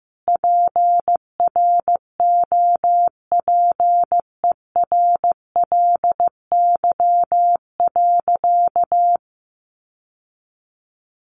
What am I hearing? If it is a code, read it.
Morse code, heard as a keyed tone, about 15 wpm, 700 Hz. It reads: PROPERLY.
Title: Sentences from Top 1100 Words 15wpm